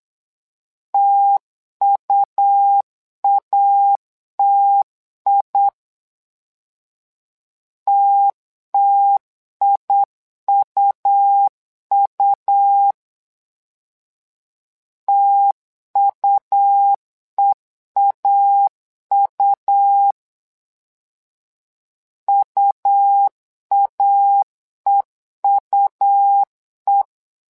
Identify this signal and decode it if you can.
Morse code, heard as a keyed tone, about 8 wpm, 800 Hz. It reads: TUATI TTIUU TUEAU UAEUE
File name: LEZIONI / 2 UHDN / 2 U 8wpm 2028